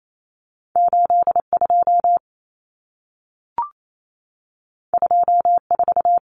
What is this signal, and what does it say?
Morse code, heard as a keyed tone, about 28 wpm, 700 Hz. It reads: 82 24
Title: Common Cut Numbers - 2-Digit 28wpm